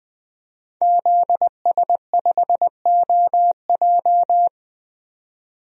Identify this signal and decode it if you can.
Morse code, heard as a keyed tone, about 20 wpm, 700 Hz. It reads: ZS5OJ
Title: DXCC-100 Call Signs - 5 Characters - 20wpm